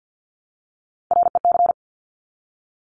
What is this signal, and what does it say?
Morse code, heard as a keyed tone, about 50 wpm, 690 Hz. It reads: REC
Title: Top Three Letter Combinations in Words 50wpm